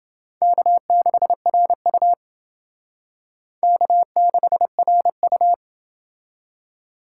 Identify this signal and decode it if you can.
Morse code, heard as a keyed tone, about 29 wpm, 700 Hz. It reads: K6RU K6RU